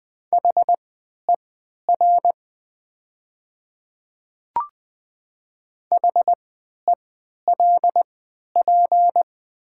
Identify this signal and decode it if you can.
Morse code, heard as a keyed tone, about 20 wpm, spaced as for 11 wpm, 700 Hz. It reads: HER HELP
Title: 3x Character Spacing - Top 500 Words - 20wpm